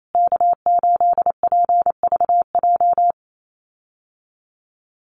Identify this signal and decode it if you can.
Morse code, heard as a keyed tone, about 28 wpm, 700 Hz. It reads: K8PVJ
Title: USA Call Signs Repeated 3x 28wpm